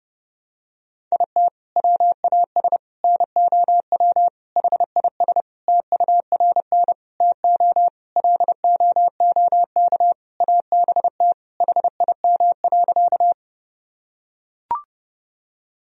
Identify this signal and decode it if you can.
Morse code, heard as a keyed tone, about 32 wpm, 700 Hz. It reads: IT WAS NOW HIS TURN TO LOOK ABT HIM.
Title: Sentences from Top 200 Words and Abbreviations 30wpm